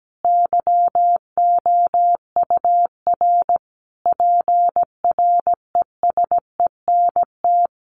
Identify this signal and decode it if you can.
Morse code, heard as a keyed tone, about 17 wpm, 700 Hz. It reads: YOUR PRESENT
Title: Sentences from Top 1200 Words 17wpm